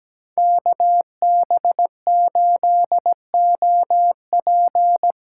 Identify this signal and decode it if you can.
Morse code, heard as a keyed tone, about 17 wpm, 700 Hz. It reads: KB8OP
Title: USA Call Signs - 8x Character Spacing - 17wpm